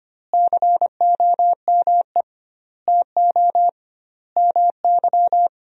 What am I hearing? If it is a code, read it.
Morse code, heard as a keyed tone, about 25 wpm, 700 Hz. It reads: COME TO MY
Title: 2x Word-Spacing - Sentences from Top 400 Words - 25wpm